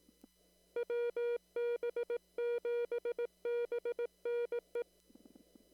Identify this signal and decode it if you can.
Morse code, heard as a keyed tone, about 18 wpm, 480 Hz. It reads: WB7BNE
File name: CW / kt5j wpx cw-op-k5pi